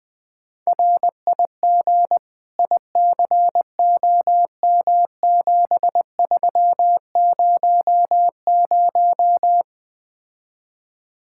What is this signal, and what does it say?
Morse code, heard as a keyed tone, about 20 wpm, 700 Hz. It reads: RIG ICOM7300
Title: Speed-Racing Common QSO Phrases - 17wpm